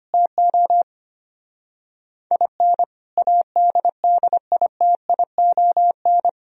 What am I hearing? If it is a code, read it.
Morse code, heard as a keyed tone, about 25 wpm, 700 Hz. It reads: TO IN ADDITION
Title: Speed-Racing - Sets of 3 Words 25wpm